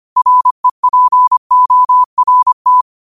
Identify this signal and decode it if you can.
Morse code, heard as a keyed tone, about 25 wpm, 1000 Hz. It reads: REPORT
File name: 25wpm